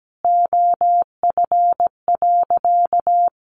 Morse code, heard as a keyed tone, about 17 wpm, 700 Hz. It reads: OF.
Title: Sentences from Top 300 Words 17wpm